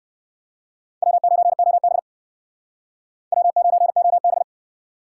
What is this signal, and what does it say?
Morse code, heard as a keyed tone, about 56 wpm, 700 Hz. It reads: W9OB W9OB